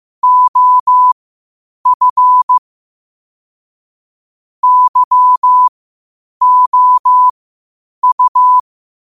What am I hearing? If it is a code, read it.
Morse code, heard as a keyed tone, about 15 wpm, spaced as for 8 wpm, 1000 Hz. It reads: OF YOU
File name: qotdfiles / 8wpm